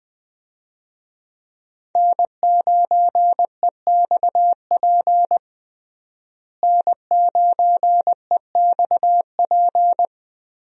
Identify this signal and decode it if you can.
Morse code, heard as a keyed tone, about 20 wpm, 700 Hz. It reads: N9EXP N9EXP
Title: USA Call Signs - 5 Characters - Repeated 3x - 20wpm